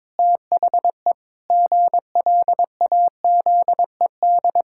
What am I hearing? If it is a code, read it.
Morse code, heard as a keyed tone, about 22 wpm, 700 Hz. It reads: THE GLAZED
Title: Pangram Challenge v2 - 22wpm